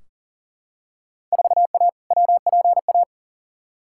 Morse code, heard as a keyed tone, about 40 wpm, 700 Hz. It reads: 4A WPA
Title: Field Day Exchanges Encore - 40wpm